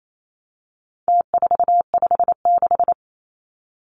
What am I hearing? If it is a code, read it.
Morse code, heard as a keyed tone, about 28 wpm, 700 Hz. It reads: T456
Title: Numbers - Contest Style - 4-Digit 28wpm